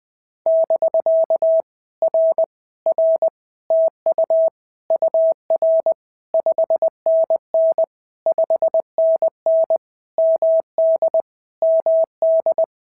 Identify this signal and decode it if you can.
Morse code, heard as a keyed tone, about 20 wpm, 650 Hz. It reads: <BK> R R TU UR 5NN 5NN MD MD
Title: POTA QSO Practice - 20wpm